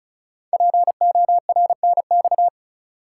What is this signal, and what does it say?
Morse code, heard as a keyed tone, about 36 wpm, 690 Hz. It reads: PORNX
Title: Code Groups - 5 Letters - 35wpm